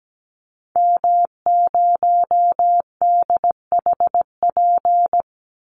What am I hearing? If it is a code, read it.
Morse code, heard as a keyed tone, about 17 wpm, 700 Hz. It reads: M0DHP
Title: Call Signs Repeated 2x - 17wpm